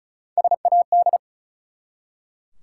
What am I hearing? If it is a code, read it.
Morse code, heard as a keyed tone, about 36 wpm, 700 Hz. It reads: SAD
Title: Three Letter Words - 35wpm